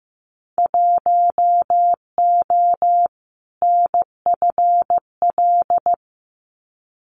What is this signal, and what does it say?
Morse code, heard as a keyed tone, about 15 wpm, 700 Hz. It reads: 1O NFL
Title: Winter Field Day Exchanges 15wpm